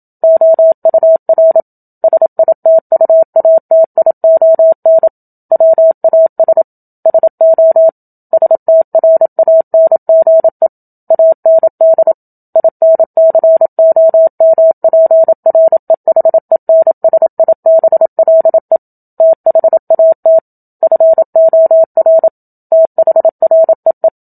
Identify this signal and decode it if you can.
Morse code, heard as a keyed tone, about 27 wpm, 650 Hz. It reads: OUR SITUATION WAS SO STRANGE AND INCOMPREHENSIBLE THAT FOR THREE
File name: Book Two / 01-Chapter 1 - 27 WPM